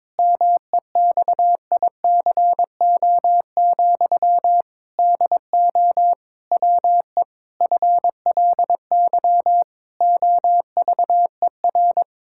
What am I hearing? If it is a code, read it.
Morse code, heard as a keyed tone, about 22 wpm, 700 Hz. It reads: MEXICO, DO WE FLY OVER
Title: Pangram Challenge v2 - 22wpm